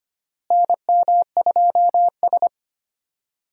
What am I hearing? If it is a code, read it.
Morse code, heard as a keyed tone, about 25 wpm, 700 Hz. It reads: NM2S
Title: USA Call Signs - 4 Characters - Repeated 3x - 25wpm